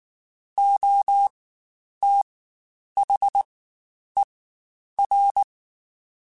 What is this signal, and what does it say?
Morse code, heard as a keyed tone, about 19 wpm, spaced as for 9 wpm, 800 Hz. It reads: OTHER
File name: Spaced / Chuckles Spaced-mp3s / HumanErr sp @18